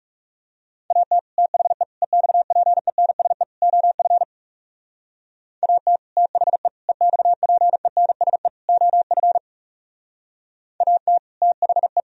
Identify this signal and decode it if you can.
Morse code, heard as a keyed tone, about 37 wpm, 700 Hz. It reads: AT THE EXPENSE OF AT THE EXPENSE OF AT THE
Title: Speed-Racing - Sets of 4 Words 30wpm